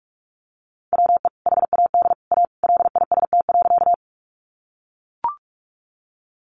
Morse code, heard as a keyed tone, about 44 wpm, 700 Hz. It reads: WE HAD A LIST.
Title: Sentences from Top 400 Words 45wpm